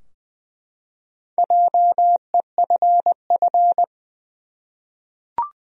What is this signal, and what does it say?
Morse code, heard as a keyed tone, about 20 wpm, 700 Hz. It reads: JEFF
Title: USA Common Names - 20wpm